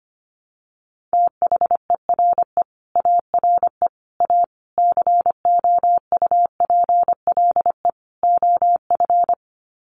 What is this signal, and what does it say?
Morse code, heard as a keyed tone, about 25 wpm, 690 Hz. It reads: THERE ARE A COUPLE OF
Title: Sets of 5 Words Encore 25wpm